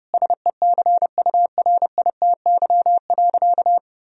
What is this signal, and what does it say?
Morse code, heard as a keyed tone, about 30 wpm, 700 Hz. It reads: SECURITY.
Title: Sentences from Top 1000 Words - Review Part 4 - 30wpm